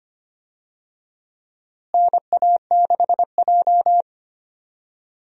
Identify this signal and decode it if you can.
Morse code, heard as a keyed tone, about 25 wpm, 700 Hz. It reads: NA6J
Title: USA Call Signs - 4 Characters - Repeated 3x - 25wpm